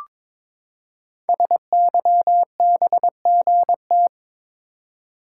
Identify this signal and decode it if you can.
Morse code, heard as a keyed tone, about 22 wpm, 700 Hz. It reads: SYBGT
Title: Code Groups - 5 Letters - 22wpm